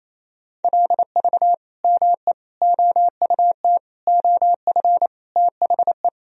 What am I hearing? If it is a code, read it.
Morse code, heard as a keyed tone, about 28 wpm, 700 Hz. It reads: LV ME OUT OF THE
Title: Sentences from Top 400 Words and Abbreviations 28wpm